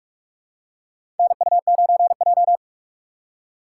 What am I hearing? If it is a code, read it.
Morse code, heard as a keyed tone, about 46 wpm, 700 Hz. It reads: NU9J
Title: USA Call Signs - 4 Characters - 6x Character Spacing - 45wpm